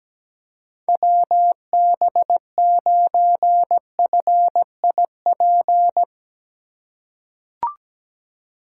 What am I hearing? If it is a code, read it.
Morse code, heard as a keyed tone, about 17 wpm, 700 Hz. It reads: WB9FIP
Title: K1USN SST QSO Practice - 17wpm